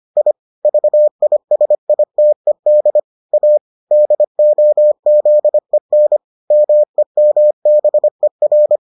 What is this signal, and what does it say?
Morse code, heard as a keyed tone, about 25 wpm, 600 Hz. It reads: I VISITED A DOZEN MEMBER